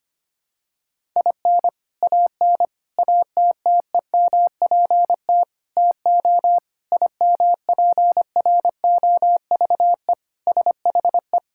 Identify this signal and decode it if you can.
Morse code, heard as a keyed tone, about 25 wpm, 700 Hz. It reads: IN AN ATTEMPT TO IMPROVE SHE
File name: practice / 25wpm / Sentences with 4-Word Phrases - 25wpm